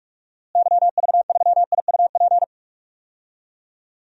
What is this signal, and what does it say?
Morse code, heard as a keyed tone, about 43 wpm, 700 Hz. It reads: YV3IUP
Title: DXCC-100 Call Signs - 6 Characters - Repeated 3x - 45wpm